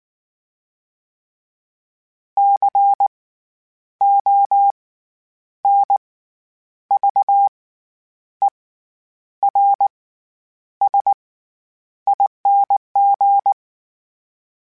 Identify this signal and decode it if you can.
Morse code, heard as a keyed tone, about 19 wpm, spaced as for 9 wpm, 800 Hz. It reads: CONVERSING
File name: Special INGs / Special INGs-mp3s / SlowToFaster ing @18